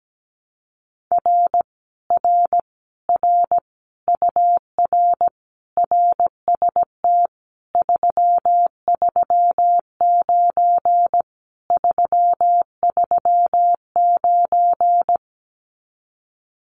Morse code, heard as a keyed tone, about 17 wpm, 700 Hz. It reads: R R R UR RST 339 339
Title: Common QSO Phrases Encore 17wpm